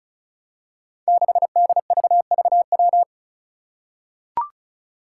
Morse code, heard as a keyed tone, about 36 wpm, 700 Hz. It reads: 6DVVW
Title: Five Challenging Letter-Number Combinations 35wpm